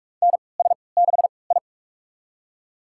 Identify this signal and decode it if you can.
Morse code, heard as a keyed tone, about 47 wpm, spaced as for 26 wpm, 700 Hz. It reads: NS6I
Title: USA Call Signs - 4 Characters - 3x Character Spacing - 45wpm